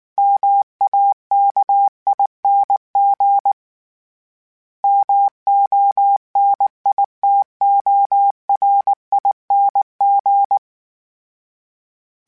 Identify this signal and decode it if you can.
Morse code, heard as a keyed tone, about 19 wpm, 800 Hz. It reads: MAKING MONITORING
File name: Special INGs / Special INGs-mp3s / Sngl Ing @18